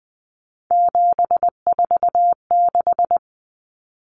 Morse code, heard as a keyed tone, about 20 wpm, 700 Hz. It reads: 746
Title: Common Cut Numbers - 3-Digit 20wpm